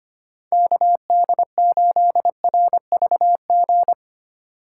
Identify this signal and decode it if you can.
Morse code, heard as a keyed tone, about 25 wpm, 700 Hz. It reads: KD8RVG
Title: USA Call Signs - 6 Characters - 8x Character Spacing - 25wpm